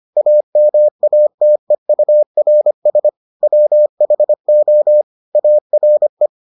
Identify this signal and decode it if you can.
Morse code, heard as a keyed tone, about 25 wpm, 600 Hz. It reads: AMATEURS WHO ARE